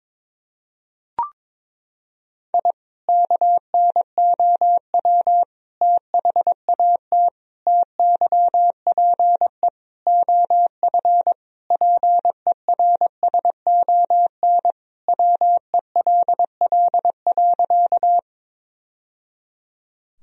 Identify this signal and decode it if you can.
Morse code, heard as a keyed tone, about 23 wpm, 700 Hz. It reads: I KNOW THAT TYPE OF PERSON WELL.
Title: Sentences from Top 1000 Words - Review Part 3 - 22wpm